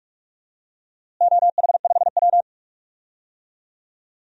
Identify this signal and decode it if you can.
Morse code, heard as a keyed tone, about 46 wpm, 700 Hz. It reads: OH5W
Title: DXCC-100 Call Signs - 4 Characters - Repeated 3x - 45wpm